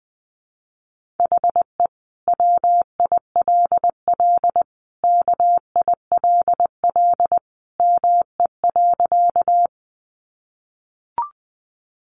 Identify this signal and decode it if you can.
Morse code, heard as a keyed tone, about 19 wpm, 700 Hz. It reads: HE WILL KILL ME.
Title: Sentences from Top 1400 Words 20wpm